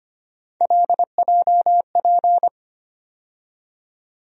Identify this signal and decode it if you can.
Morse code, heard as a keyed tone, about 25 wpm, 700 Hz. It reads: LJP
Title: USA Call Sign Postfixes - 3 Letters 25wpm